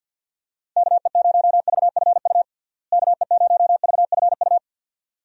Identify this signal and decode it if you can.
Morse code, heard as a keyed tone, about 50 wpm, 700 Hz. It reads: KE0VFU KE0VFU
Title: USA Call Signs - 6 Characters - Repeated 2x - 50wpm